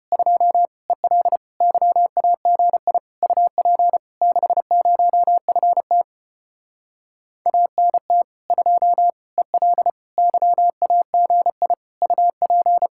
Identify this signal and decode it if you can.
Morse code, heard as a keyed tone, about 33 wpm, 700 Hz. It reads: 2 EL YAGI UP 60FT ANT 2 EL YAGI UP